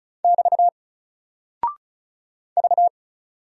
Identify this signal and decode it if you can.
Morse code, heard as a keyed tone, about 35 wpm, 700 Hz. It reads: <BT> V
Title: Single Character (Letter:Number:Punctuation:Prosign) 35wpm